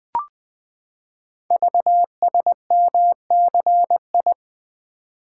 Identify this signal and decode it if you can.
Morse code, heard as a keyed tone, about 21 wpm, 700 Hz. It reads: VSMCI
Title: Code Groups - 5 Letter:Numbers - 20wpm